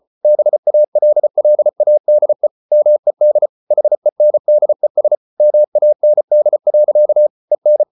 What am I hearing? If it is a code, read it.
Morse code, heard as a keyed tone, about 34 wpm, 600 Hz. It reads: BALLADE MED HENDES MAND. EN
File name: Ferie QRQ 34wpm